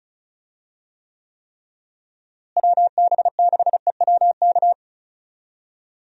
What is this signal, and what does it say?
Morse code, heard as a keyed tone, about 36 wpm, 700 Hz. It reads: WB6EWK
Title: USA Call Signs - 6 Characters - Repeated 3x - 35wpm